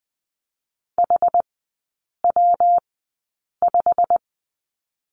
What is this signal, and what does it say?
Morse code, heard as a keyed tone, about 20 wpm, spaced as for 8 wpm, 700 Hz. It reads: HW5
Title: 178 Lesson 29e - Slash - Three Characters 20wpm